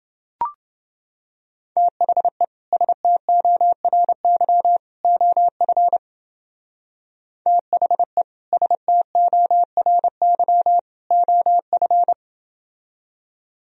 Morse code, heard as a keyed tone, about 28 wpm, 700 Hz. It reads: THE STORY OF THE STORY OF